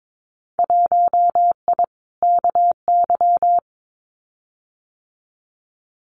Morse code, heard as a keyed tone, about 22 wpm, 700 Hz. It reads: 1I KY
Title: Winter Field Day Exchanges 22wpm